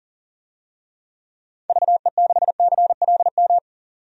Morse code, heard as a keyed tone, about 40 wpm, 700 Hz. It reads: VE6CLM
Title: DXCC-100 Call Signs - 6 Characters - 40wpm